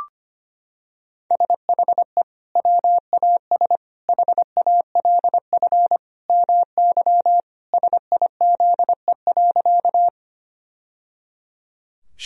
Morse code, heard as a keyed tone, about 25 wpm, 700 Hz. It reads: SHE WAS HALF MY SIZE.
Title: Sentences from Top 1000 Words - Review Part 2 - 25wpm